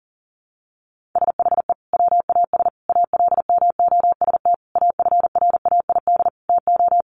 Morse code, heard as a keyed tone, about 40 wpm, 700 Hz. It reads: SHE WAS ALMOST AFRAID TO
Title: Sentences from Top 1500 Words 40wpm